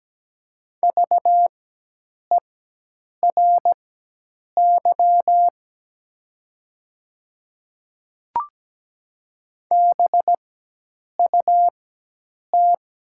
Morse code, heard as a keyed tone, about 17 wpm, spaced as for 8 wpm, 700 Hz. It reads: VERY BUT
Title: 4x Character Spacing - Top 300 Words - 17wpm